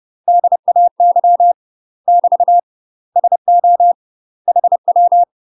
Morse code, heard as a keyed tone, about 30 wpm, 700 Hz. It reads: DAY <BT> SO HW
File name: UK MASTER QSOs TRAINING / 30WPM / UK MASTER QSOs 0026